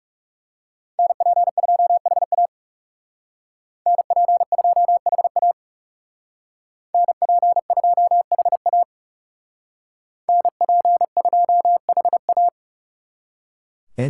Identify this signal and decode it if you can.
Morse code, heard as a keyed tone, about 37 wpm, 700 Hz. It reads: NP2HA NP2HA NP2HA NP2HA
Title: Speed Racing - USA 2x2 Call Signs 30wpm